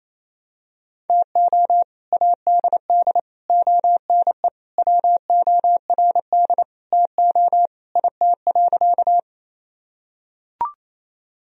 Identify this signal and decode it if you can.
Morse code, heard as a keyed tone, about 28 wpm, 700 Hz. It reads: TO ADD ONE WORD TO IT.
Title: Sentences from Top 1000 Words - Review Part 1 - 28wpm